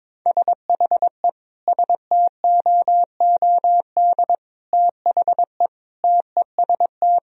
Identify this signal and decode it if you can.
Morse code, heard as a keyed tone, about 22 wpm, 700 Hz. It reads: SHE STOOD THE TEST
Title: Sentences from Top 1000 Words - Review Part 3 - 22wpm